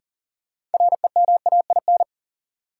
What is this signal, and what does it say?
Morse code, heard as a keyed tone, about 40 wpm, 700 Hz. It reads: REMAIN
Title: Sentences from Top 1200 Words - New Word Drill - 40wpm